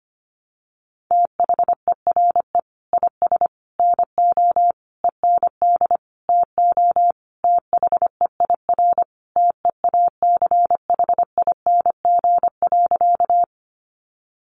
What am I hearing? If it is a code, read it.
Morse code, heard as a keyed tone, about 25 wpm, 700 Hz. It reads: THERE IS NO END TO THEIR TEACHING.
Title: Sentences from Top 1100 Words 25wpm